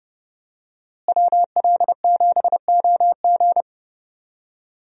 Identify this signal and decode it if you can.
Morse code, heard as a keyed tone, about 30 wpm, 700 Hz. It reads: WL7OG
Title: USA Call Signs - 5 Characters - Repeated 2x - 30wpm